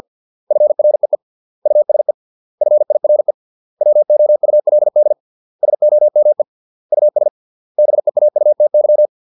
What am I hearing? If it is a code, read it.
Morse code, heard as a keyed tone, about 50 wpm, 600 Hz. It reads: FREE USE FIRE WOULD SOME US BEAUTY